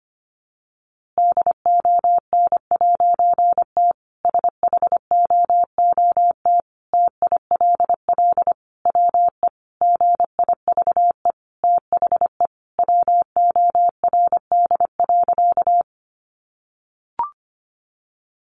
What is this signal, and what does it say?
Morse code, heard as a keyed tone, about 25 wpm, 700 Hz. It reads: DON'T SHOOT TILL WE GIVE THE WORD.
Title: A Sound Of Thunder 25wpm